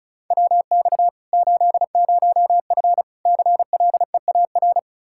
Morse code, heard as a keyed tone, about 36 wpm, 700 Hz. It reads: WX 80F CLEAR
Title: Practice QSO - Exchange 2 - Info - 35wpm